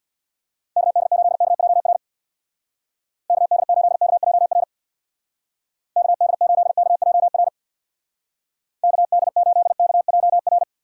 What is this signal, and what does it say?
Morse code, heard as a keyed tone, about 65 wpm, 700 Hz. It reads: KD8KJR KD8KJR KD8KJR KD8KJR